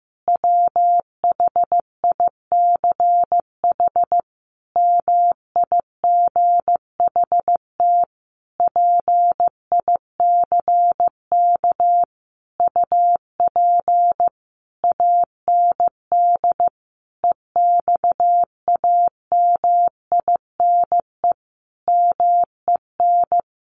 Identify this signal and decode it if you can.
Morse code, heard as a keyed tone, about 14 wpm, 700 Hz. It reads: WHICH MIGHT PICK UP AND EXAMINE MEN
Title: A Sound Of Thunder 15wpm